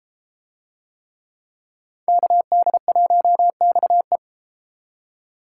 Morse code, heard as a keyed tone, about 32 wpm, 700 Hz. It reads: KD1XE